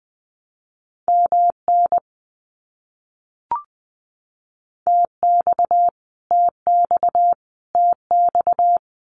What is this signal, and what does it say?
Morse code, heard as a keyed tone, about 20 wpm, 700 Hz. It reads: MN TX TX TX
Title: US States Repeated 3x 20wpm